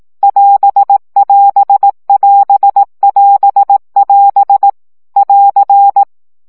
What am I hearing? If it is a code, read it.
Morse code, heard as a keyed tone, about 18 wpm, 800 Hz. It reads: <AS><AS><AS><AS><AS> <AR>
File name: CW gyakorlas as 15 baud